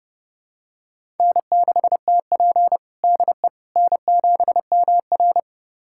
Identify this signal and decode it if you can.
Morse code, heard as a keyed tone, about 30 wpm, 700 Hz. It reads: N6TP DE N7MR
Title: USA Callsign DE Callsign - 4 Characters - 30wpm